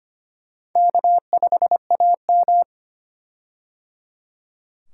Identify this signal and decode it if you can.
Morse code, heard as a keyed tone, about 25 wpm, 700 Hz. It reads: K5AM
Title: K1USN SST QSO Practice - 25wpm